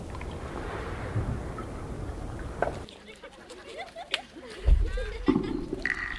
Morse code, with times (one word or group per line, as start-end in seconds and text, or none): none